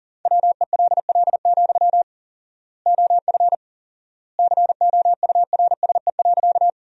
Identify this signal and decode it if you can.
Morse code, heard as a keyed tone, about 40 wpm, 700 Hz. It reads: WELL, OF COURSE.
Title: 4x Word-Spacing - Sentences from Top 500 Words - 40wpm